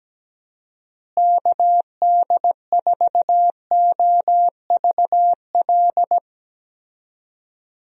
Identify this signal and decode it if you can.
Morse code, heard as a keyed tone, about 17 wpm, 700 Hz. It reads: KD4OVL
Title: USA Call Signs - 6 Characters - Repeated 3x - 17wpm